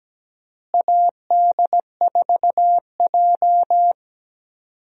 Morse code, heard as a keyed tone, about 17 wpm, 700 Hz. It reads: AD4J
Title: USA Call Signs - 4 Characters - Repeated 2x - 17wpm